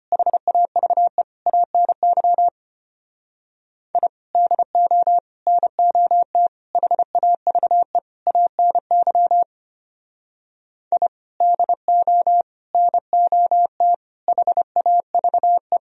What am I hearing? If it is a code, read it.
Morse code, heard as a keyed tone, about 30 wpm, 700 Hz. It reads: HAVE ANY I DO NOT HAVE ANY I DO NOT HAVE
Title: Speed-Racing - Sets of 5 Words 25wpm